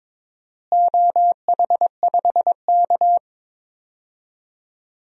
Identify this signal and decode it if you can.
Morse code, heard as a keyed tone, about 22 wpm, 700 Hz. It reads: OH5K
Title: DXCC-100 Call Signs - 4 Characters - Repeated 2x - 22wpm